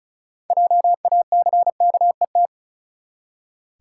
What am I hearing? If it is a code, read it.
Morse code, heard as a keyed tone, about 36 wpm, 700 Hz. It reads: JACKET
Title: Six Letter Words - 35wpm